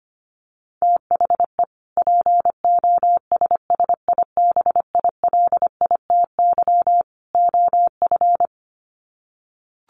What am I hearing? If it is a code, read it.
Morse code, heard as a keyed tone, about 25 wpm, 700 Hz. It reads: THE POSSIBILITY OF
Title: Sets of 3 Words 25wpm